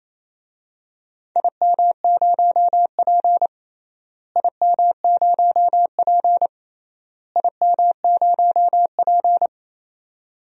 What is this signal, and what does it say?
Morse code, heard as a keyed tone, about 28 wpm, 700 Hz. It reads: IM0P IM0P IM0P
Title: DXCC-100 Call Signs - 4 Characters - Repeated 3x - 28wpm